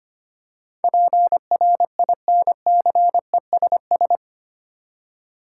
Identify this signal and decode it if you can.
Morse code, heard as a keyed tone, about 25 wpm, 700 Hz. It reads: PRINCESS
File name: Eight Letter Words - 25wpm